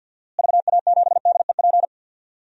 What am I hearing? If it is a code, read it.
Morse code, heard as a keyed tone, about 53 wpm, 700 Hz. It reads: VA7DEP
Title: DXCC-100 Call Signs - 6 Characters - Repeated 3x - 50wpm